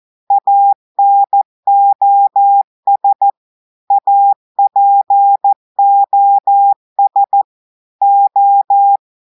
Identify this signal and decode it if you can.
Morse code, heard as a keyed tone, about 14 wpm, 800 Hz. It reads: ANOS APOS O
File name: audios / portugais / portugais texte en clair 1 vitesse 14 mots minutes